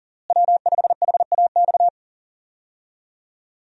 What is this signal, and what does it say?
Morse code, heard as a keyed tone, about 37 wpm, 700 Hz. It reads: W5HAX
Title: USA Call Signs - 5 Characters - Repeated 3x - 40wpm